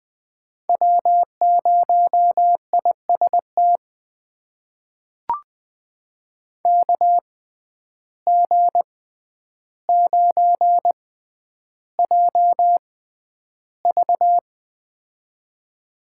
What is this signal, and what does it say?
Morse code, heard as a keyed tone, about 20 wpm, spaced as for 10 wpm, 700 Hz. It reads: W0IST KG9JV
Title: USA Call Signs - 5 Characters - 6x Character Spacing - 20wpm